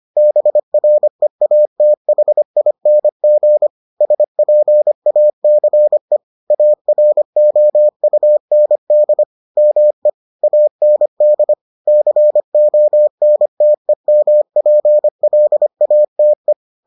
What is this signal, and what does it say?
Morse code, heard as a keyed tone, about 25 wpm, 600 Hz. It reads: BREATHING SPACE AROUND ME AND CONTEMPLATE